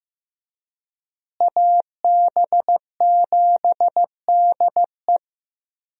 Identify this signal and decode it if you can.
Morse code, heard as a keyed tone, about 16 wpm, 700 Hz. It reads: AB7DE
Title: USA Call Signs - 5 Characters - Repeated 3x - 15wpm